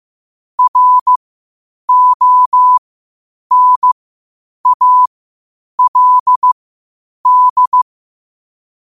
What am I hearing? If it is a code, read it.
Morse code, heard as a keyed tone, about 15 wpm, spaced as for 8 wpm, 1000 Hz. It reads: RONALD
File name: qotdfiles / 8wpm